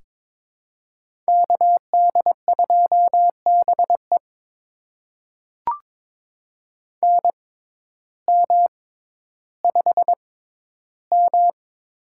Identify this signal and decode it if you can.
Morse code, heard as a keyed tone, about 22 wpm, spaced as for 12 wpm, 700 Hz. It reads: KD2BE NM5M
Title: USA Call Signs - 6x Character Spacing - 22wpm